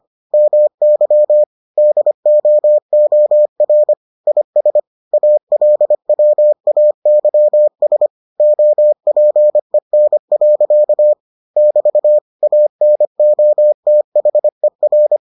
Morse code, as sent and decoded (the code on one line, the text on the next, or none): -- -.-- / -.. --- --- .-. / .. ... / .- .-.. .-- .- -.-- ... / --- .--. -. .-.-.- / -...- / .- -. --- - .... .-.
MY DOOR IS ALWAYS OPEN. <BT> ANOTHER